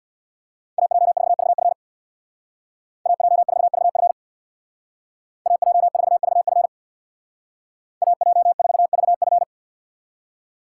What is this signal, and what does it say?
Morse code, heard as a keyed tone, about 64 wpm, 700 Hz. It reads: AJ4VF AJ4VF AJ4VF AJ4VF